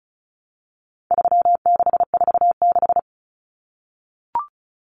Morse code, heard as a keyed tone, about 34 wpm, 700 Hz. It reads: 3646
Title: Common Cut Numbers - 4-Digit 35wpm